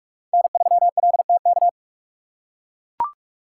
Morse code, heard as a keyed tone, about 44 wpm, 700 Hz. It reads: N3LTK